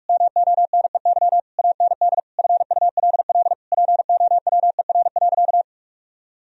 Morse code, heard as a keyed tone, about 46 wpm, 700 Hz. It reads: MONEY AND FULL POWER.
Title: Sentences from Top 1000 Words - Review Part 3 - 45wpm